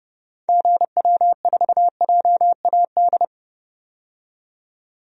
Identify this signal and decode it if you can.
Morse code, heard as a keyed tone, about 30 wpm, 700 Hz. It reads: GW4JAD
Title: DXCC-100 Call Signs - 6 Characters - 30wpm